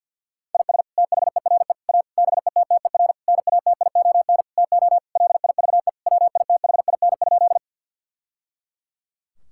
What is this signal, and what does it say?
Morse code, heard as a keyed tone, about 50 wpm, 700 Hz. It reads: IS THERE A BETTER NATION TO LIVE WITHIN?
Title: Sentences from Top 1000 Words - Review Part 6 - 50wpm